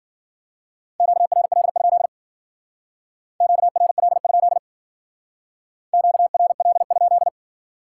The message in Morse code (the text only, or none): QRL? QRL? QRL?